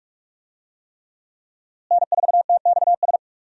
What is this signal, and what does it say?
Morse code, heard as a keyed tone, about 41 wpm, 700 Hz. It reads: N4TXS
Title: USA Call Signs - 5 Characters - Repeated 3x - 45wpm